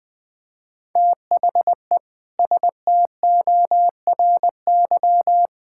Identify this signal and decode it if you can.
Morse code, heard as a keyed tone, about 21 wpm, 700 Hz. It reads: THE STORY
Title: Sentences from Top 1000 Words - Review Part 6 - 20wpm